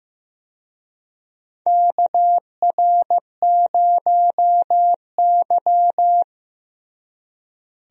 Morse code, heard as a keyed tone, about 15 wpm, 700 Hz. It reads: KR0Y
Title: USA Call Signs - 4 Characters - Repeated 2x - 15wpm